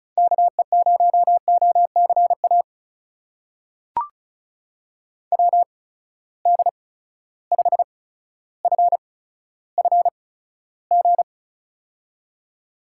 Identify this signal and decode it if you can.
Morse code, heard as a keyed tone, about 36 wpm, spaced as for 15 wpm, 700 Hz. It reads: KE0OCA WD5FFG
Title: USA Call Signs - 6 Characters - 8x Character Spacing - 35wpm